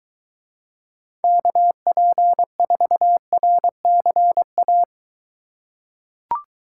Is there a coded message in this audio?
Yes: Morse code from 1.2 to 4.8 s, heard as a keyed tone, about 22 wpm, 700 Hz. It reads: KP4RCA